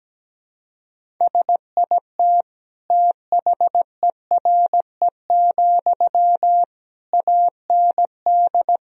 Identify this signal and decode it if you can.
Morse code, heard as a keyed tone, about 17 wpm, 700 Hz. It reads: SIT THERE, AND
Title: Sentences from Top 1000 Words - Review Part 1 - 17wpm